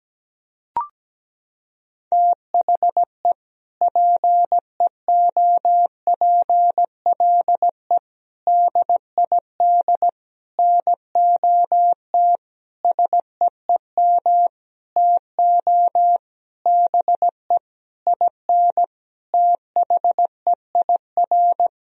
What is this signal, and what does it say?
Morse code, heard as a keyed tone, about 17 wpm, 700 Hz. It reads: THE PEOPLE DID NOT SEEM TO BE IN THEIR
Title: Sentences from Top 1000 Words - Review Part 1 - 17wpm